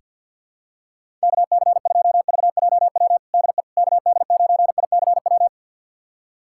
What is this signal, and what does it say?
Morse code, heard as a keyed tone, about 50 wpm, 700 Hz. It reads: KC2VJW DE KD9ICW
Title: USA Callsign DE Callsign 50wpm